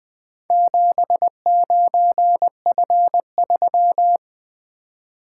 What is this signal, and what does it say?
Morse code, heard as a keyed tone, about 20 wpm, 700 Hz. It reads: 79F3
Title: Four Letter-Number Combinations - 20wpm